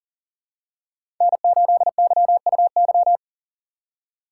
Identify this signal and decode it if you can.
Morse code, heard as a keyed tone, about 40 wpm, 700 Hz. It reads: N8YUY
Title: USA Call Signs - 5 Characters - 3x Character Spacing - 40wpm